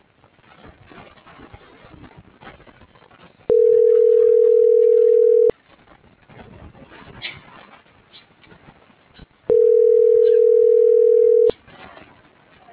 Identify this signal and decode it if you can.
Ringback tone (North America), 440 + 480 Hz.